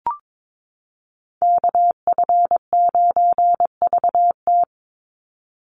Morse code, heard as a keyed tone, about 22 wpm, 700 Hz. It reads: KF9VT
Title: USA Call Signs 22wpm